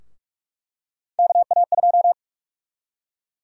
Morse code, heard as a keyed tone, about 46 wpm, 700 Hz. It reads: KA2
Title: USA Call Sign Prefixes - 45wpm